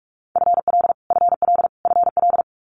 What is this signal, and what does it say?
Morse code, heard as a keyed tone, about 46 wpm, 700 Hz. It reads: FL FL FL
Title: Multi-letter Phonemes Repeated 3x 45wpm